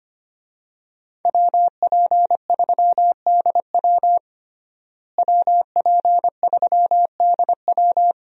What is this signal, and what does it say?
Morse code, heard as a keyed tone, about 25 wpm, 700 Hz. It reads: WP3DW WP3DW
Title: USA Call Signs - 5 Characters - Repeated 2x - 25wpm